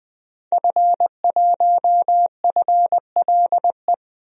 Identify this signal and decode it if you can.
Morse code, heard as a keyed tone, about 20 wpm, 700 Hz. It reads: F1FLE
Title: DXCC-100 Call Signs - 5 Characters - Repeated 2x - 20wpm